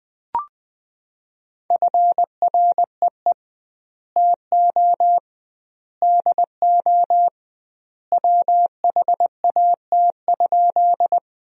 Morse code, heard as a keyed tone, about 21 wpm, 700 Hz. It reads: FREE TO DO WHAT?
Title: 2x Word-Spacing - Sentences from Top 500 Words - 20wpm